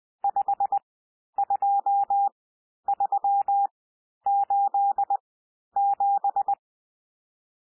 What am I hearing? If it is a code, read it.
Morse code, heard as a keyed tone, about 20 wpm, spaced as for 11 wpm, 800 Hz. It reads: 52387
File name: cwmp3 / N1 SOUND NUMBERS 20 wpm